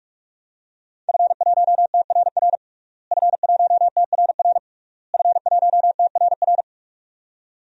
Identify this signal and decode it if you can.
Morse code, heard as a keyed tone, about 46 wpm, 700 Hz. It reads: F1TRR F1TRR F1TRR
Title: DXCC-100 Call Signs - 5 Characters - Repeated 3x - 45wpm